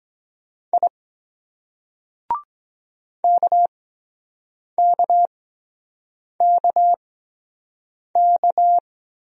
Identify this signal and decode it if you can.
Morse code, heard as a keyed tone, about 23 wpm, spaced as for 7 wpm, 700 Hz. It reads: I KKKK